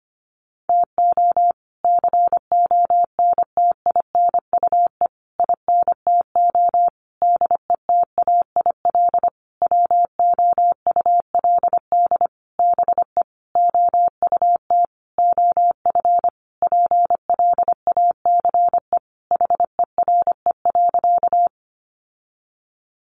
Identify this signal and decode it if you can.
Morse code, heard as a keyed tone, about 25 wpm, 700 Hz. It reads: TO CONTINUE INTO DETAIL WOULD BE OUT OF PLACE HERE.
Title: Sentences from Top 1500 Words 25wpm